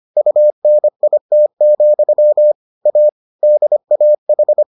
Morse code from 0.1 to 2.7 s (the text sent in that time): UNIT,